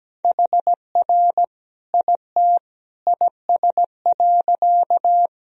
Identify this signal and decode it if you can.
Morse code, heard as a keyed tone, about 17 wpm, 700 Hz. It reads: HR IT IS.
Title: Sentences from Top 200 Words and Abbreviations 17wpm